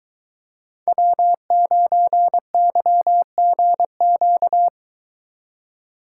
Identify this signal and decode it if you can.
Morse code, heard as a keyed tone, about 24 wpm, 700 Hz. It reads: W9YGQ